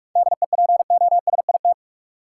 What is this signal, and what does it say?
Morse code, heard as a keyed tone, about 46 wpm, 700 Hz. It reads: DEPOSIT
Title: Sentences from Top 1300 Words - New Word Drill - 45wpm